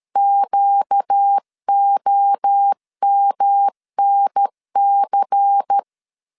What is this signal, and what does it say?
Morse code, heard as a keyed tone, about 13 wpm, 800 Hz. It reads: QOMNC
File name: CW / DET / 1 6 1500C S A 12wpm 3440